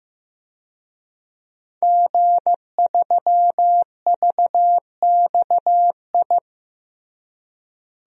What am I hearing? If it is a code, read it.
Morse code, heard as a keyed tone, about 16 wpm, 700 Hz. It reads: G3VXI
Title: DXCC-100 Call Signs - 5 Characters - Repeated 2x - 15wpm